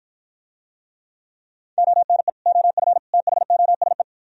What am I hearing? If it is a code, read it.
Morse code, heard as a keyed tone, about 52 wpm, 700 Hz. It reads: ONE OF THOSE